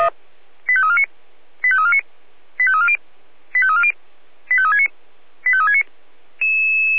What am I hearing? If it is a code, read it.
Five-tone selective call (ZVEI), heard as quick runs of five tones, tones of about 80 ms. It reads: 85349 85349 85340 85340 85369 85369